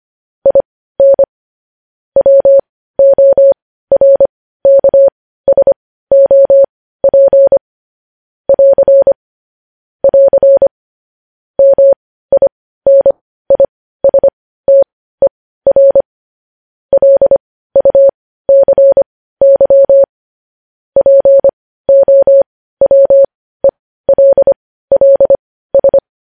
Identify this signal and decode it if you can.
Morse code, heard as a keyed tone, about 25 wpm, spaced as for 15 wpm, 550 Hz. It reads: IN WORKSOP <AR> <AR> MINISTER LUCY POWELLS
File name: podcast / News headlines-1x-25x15